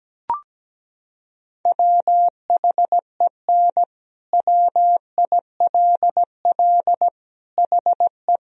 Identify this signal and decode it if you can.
Morse code, heard as a keyed tone, about 17 wpm, 700 Hz. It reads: WHEN WILL HE
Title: Sentences from Top 1000 Words - Review Part 1 - 17wpm